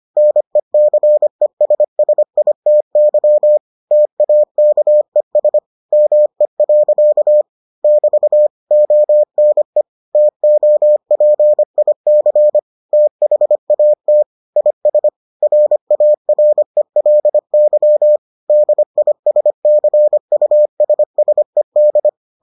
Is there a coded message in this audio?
Yes: Morse code from 0.2 to 22.1 s, heard as a keyed tone, about 25 wpm, 600 Hz. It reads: NECESSITY TAKES ME. <BT> ONE TOPIC THAT IS RARELY DISCUSSED